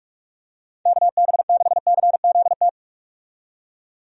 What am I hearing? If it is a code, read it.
Morse code, heard as a keyed tone, about 48 wpm, 700 Hz. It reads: KB6CZT
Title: USA Call Signs - 6 Characters - 45wpm